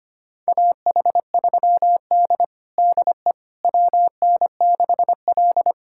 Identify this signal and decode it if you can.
Morse code, heard as a keyed tone, about 25 wpm, 700 Hz. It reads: AH3D DE WN6L
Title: USA Callsign DE Callsign - 4 Characters - 25wpm